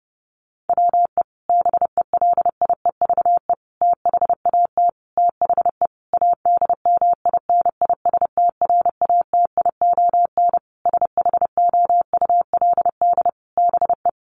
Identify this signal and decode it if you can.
Morse code, heard as a keyed tone, about 30 wpm, 700 Hz. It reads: WE BELIEVE THAT THE ADMINISTRATION SHOULD BE
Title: Sentences from Top 1300 Words 30wpm